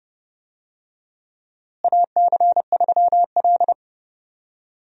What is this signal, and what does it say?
Morse code, heard as a keyed tone, about 31 wpm, 700 Hz. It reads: AC3L
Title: USA Call Signs - 4 Characters - Repeated 3x - 30wpm